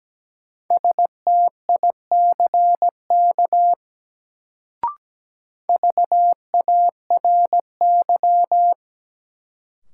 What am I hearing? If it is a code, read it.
Morse code, heard as a keyed tone, about 17 wpm, 700 Hz. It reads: STICK VARY
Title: Sentences from Top 1200 Words - New Word Drill - 17wpm